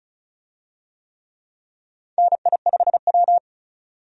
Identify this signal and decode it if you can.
Morse code, heard as a keyed tone, about 36 wpm, 700 Hz. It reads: NI5W
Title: USA Call Signs - 4 Characters - 35wpm